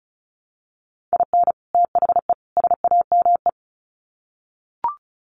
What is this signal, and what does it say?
Morse code, heard as a keyed tone, about 34 wpm, 700 Hz. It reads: IN THE SAME
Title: Sets of 3 Words 35wpm